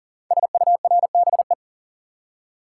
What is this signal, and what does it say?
Morse code, heard as a keyed tone, about 39 wpm, 690 Hz. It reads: SURBE
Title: Code Groups - 5 Letters - 40wpm